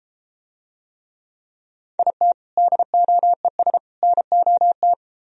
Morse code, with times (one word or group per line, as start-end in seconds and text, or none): 2.0-2.3 IT
2.6-3.8 DOES
4.0-4.9 NOT